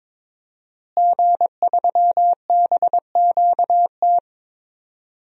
Morse code, heard as a keyed tone, about 23 wpm, 700 Hz. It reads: G3BQT
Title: DXCC-100 Call Signs - 5 Characters - Repeated 3x - 22wpm